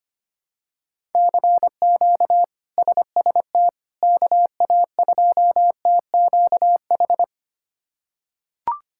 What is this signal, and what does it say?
Morse code, heard as a keyed tone, about 25 wpm, 700 Hz. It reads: CQ SST KA2TQH
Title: K1USN SST QSO Practice - 25wpm